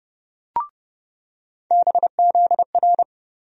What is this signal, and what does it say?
Morse code, heard as a keyed tone, about 30 wpm, 700 Hz. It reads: BZR
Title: USA Call Sign Postfixes - 3 Letters 30wpm